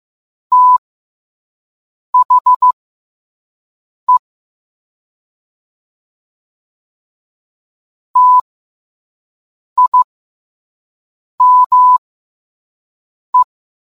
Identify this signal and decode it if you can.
Morse code, heard as a keyed tone, about 15 wpm, spaced as for 5 wpm, 1000 Hz. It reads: THE TIME